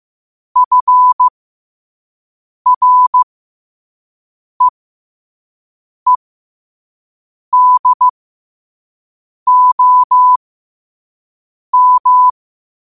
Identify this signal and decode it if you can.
Morse code, heard as a keyed tone, about 15 wpm, spaced as for 5 wpm, 1000 Hz. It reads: FREEDOM
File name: qotdfiles / 5wpm